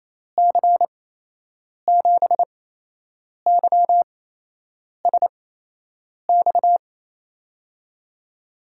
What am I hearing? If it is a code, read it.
Morse code, heard as a keyed tone, about 28 wpm, spaced as for 8 wpm, 700 Hz. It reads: C7YSX
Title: USA Call Signs - 6 Characters - 8x Character Spacing - 28wpm